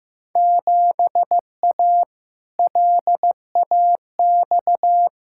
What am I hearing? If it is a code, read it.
Morse code, heard as a keyed tone, about 15 wpm, 700 Hz. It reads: 7A LAX
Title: Field Day Exchanges Encore - 15wpm